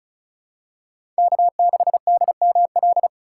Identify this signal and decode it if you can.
Morse code, heard as a keyed tone, about 36 wpm, 700 Hz. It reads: K6DML
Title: USA Call Signs - 5 Characters - 4x Character Spacing - 35wpm